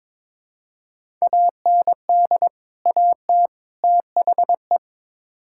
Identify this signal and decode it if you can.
Morse code, heard as a keyed tone, about 22 wpm, 700 Hz. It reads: AND AT THE
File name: practice / 22wpm / Build A Phrase - 5-Word Phrases - 22wpm